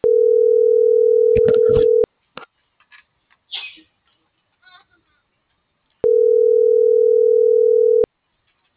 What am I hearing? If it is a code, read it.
Ringback tone (North America), 440 + 480 Hz.